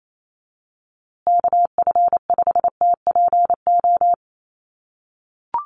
Morse code, heard as a keyed tone, about 28 wpm, 700 Hz. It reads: KF5TPO
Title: USA Call Signs Repeated 2x 28wpm